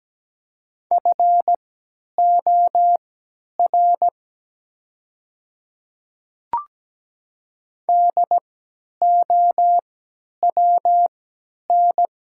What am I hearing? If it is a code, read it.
Morse code, heard as a keyed tone, about 17 wpm, spaced as for 10 wpm, 700 Hz. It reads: FOR DOWN
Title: 3x Character Spacing - Top 500 Words - 17wpm